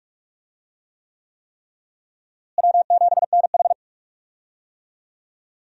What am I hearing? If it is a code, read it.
Morse code, heard as a keyed tone, about 46 wpm, 700 Hz. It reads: W7NH
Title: USA Call Signs - 4 Characters - Repeated 3x - 45wpm